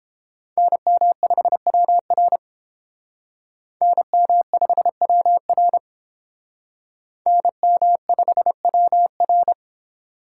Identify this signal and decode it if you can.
Morse code, heard as a keyed tone, about 31 wpm, 700 Hz. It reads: NM5WR NM5WR NM5WR